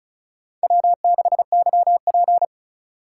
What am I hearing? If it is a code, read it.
Morse code, heard as a keyed tone, about 43 wpm, 700 Hz. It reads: W6YP
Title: Speed Racing - USA Call Signs 35wpm